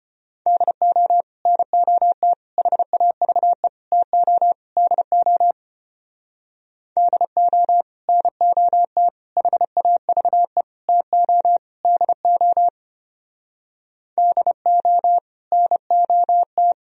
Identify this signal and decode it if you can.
Morse code, heard as a keyed tone, about 33 wpm, 700 Hz. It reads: DO NOT HAVE TO DO DO NOT HAVE TO DO DO NOT